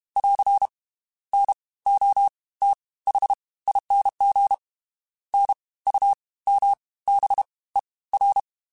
Morse code, heard as a keyed tone, about 32 wpm, 800 Hz. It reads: <AR> N O T H ING N U M B E R
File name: Special INGs / Special INGs-mp3s / Fast ing @30